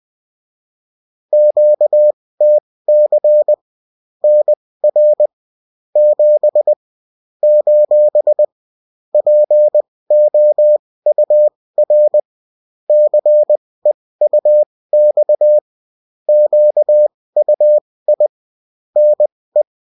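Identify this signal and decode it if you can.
Morse code, heard as a keyed tone, about 20 wpm, spaced as for 16 wpm, 600 Hz. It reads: QTC NR 7 : POUR CEUX QUI NE
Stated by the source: Morse code bulletin of a French amateur CW union, sent as text and in CW.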